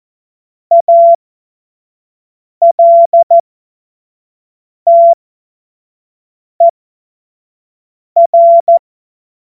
Morse code, heard as a keyed tone, about 14 wpm, spaced as for 5 wpm, 680 Hz. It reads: ALTER
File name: Morsekurs / Witze / Witz 20. Januar 2026 25 F70